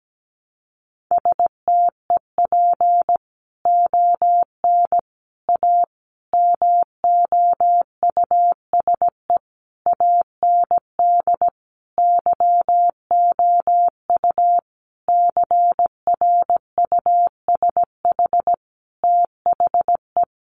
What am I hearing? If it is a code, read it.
Morse code, heard as a keyed tone, about 17 wpm, 700 Hz. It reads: STEP ON A MOUSE AND YOU CRUSH THE
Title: A Sound Of Thunder 17wpm